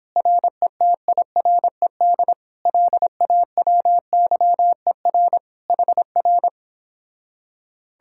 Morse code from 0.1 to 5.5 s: RETIRED LAWYER

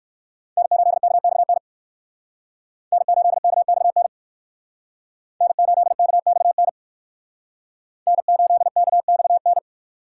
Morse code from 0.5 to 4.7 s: N8KXN N8KXN

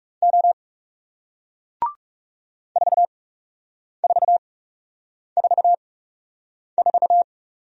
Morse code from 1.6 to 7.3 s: 4444

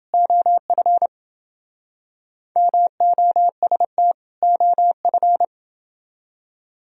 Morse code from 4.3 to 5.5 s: OF

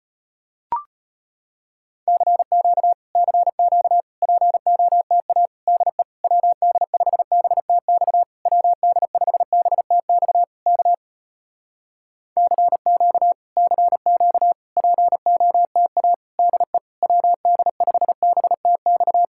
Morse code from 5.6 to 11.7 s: DE WD5BTX WD5BTX K

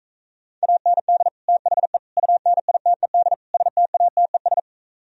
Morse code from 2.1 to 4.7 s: UNITED STATES

